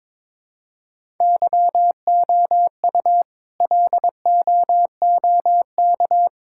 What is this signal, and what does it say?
Morse code, heard as a keyed tone, about 22 wpm, 700 Hz. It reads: YOU LOOK
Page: morse-code-ninja-practice-22wpm